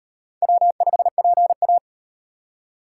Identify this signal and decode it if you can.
Morse code, heard as a keyed tone, about 38 wpm, 700 Hz. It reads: W5PA